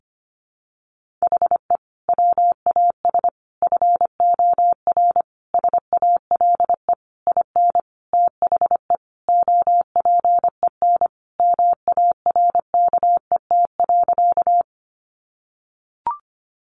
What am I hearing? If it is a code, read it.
Morse code, heard as a keyed tone, about 25 wpm, 700 Hz. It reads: HE WAS FOR SALE IN THE OPEN MARKET.
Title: Sentences from Top 1400 Words 25wpm